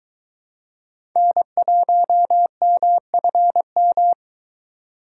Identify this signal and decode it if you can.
Morse code, heard as a keyed tone, about 22 wpm, 700 Hz. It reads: N1MFM